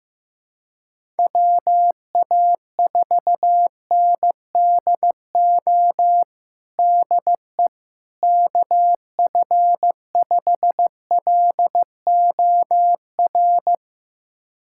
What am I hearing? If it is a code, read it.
Morse code, heard as a keyed tone, about 15 wpm, 700 Hz. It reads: WA4NDO DE KF5LOR
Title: USA Callsign DE Callsign 15wpm